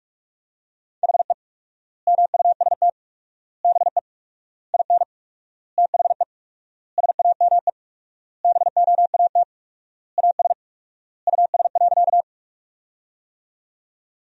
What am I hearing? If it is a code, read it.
Morse code, heard as a keyed tone, about 47 wpm, 700 Hz. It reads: HE MUST BE IN THE SAME BOAT AS US.
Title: 4x Word-Spacing - Sentences from Top 500 Words - 45wpm